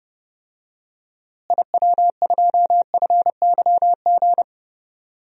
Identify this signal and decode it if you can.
Morse code, heard as a keyed tone, about 30 wpm, 700 Hz. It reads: IW2FYG
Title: DXCC-100 Call Signs - 6 Characters - Repeated 3x - 30wpm